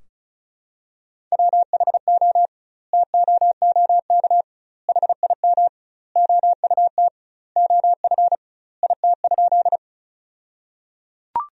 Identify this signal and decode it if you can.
Morse code, heard as a keyed tone, about 36 wpm, 700 Hz. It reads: WHO TOOK HIM OUT OF IT?
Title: 2x Word-Spacing - Sentences from Top 300 Words - 35wpm